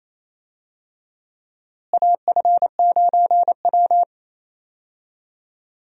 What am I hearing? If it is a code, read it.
Morse code, heard as a keyed tone, about 28 wpm, 700 Hz. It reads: AF9W
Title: USA Call Signs - 4 Characters - Repeated 3x - 28wpm